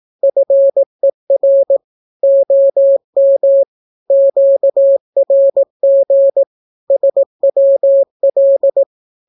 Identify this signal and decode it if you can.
Morse code, heard as a keyed tone, about 18 wpm, 550 Hz. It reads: FER OM QRG SWL
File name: naukapodstaw / skroty / skroty